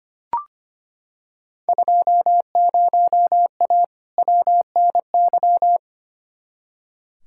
Morse code, heard as a keyed tone, about 25 wpm, 700 Hz. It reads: 20A WNY
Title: Field Day Exchanges Encore - 25wpm